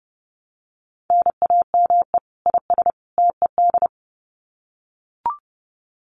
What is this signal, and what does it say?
Morse code, heard as a keyed tone, about 28 wpm, 700 Hz. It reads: NAME IS TED
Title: Common QSO Phrases 30wpm